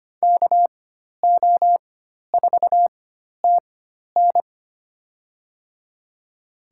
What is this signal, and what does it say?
Morse code, heard as a keyed tone, about 25 wpm, spaced as for 12 wpm, 700 Hz. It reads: KO4TN
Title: USA Call Signs - 5 Characters - 4x Character Spacing - 25wpm